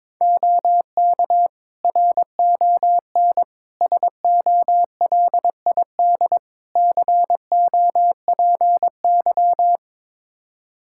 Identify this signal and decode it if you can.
Morse code, heard as a keyed tone, about 21 wpm, 700 Hz. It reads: OK RON SOLID COPY
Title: Practice QSO - Exchange 3 - More Info - 22wpm